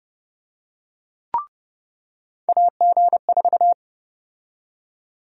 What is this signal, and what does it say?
Morse code, heard as a keyed tone, about 32 wpm, 700 Hz. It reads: AG4
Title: USA Call Sign Prefixes - 30wpm